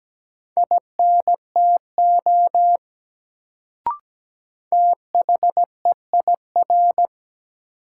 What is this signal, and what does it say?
Morse code, heard as a keyed tone, about 17 wpm, 700 Hz. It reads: INTO THEIR
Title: Sets of 2 Words - New Word Drill - 17wpm